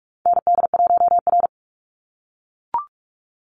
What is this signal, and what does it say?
Morse code, heard as a keyed tone, about 44 wpm, 700 Hz. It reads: ND1R
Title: Call Signs - 45wpm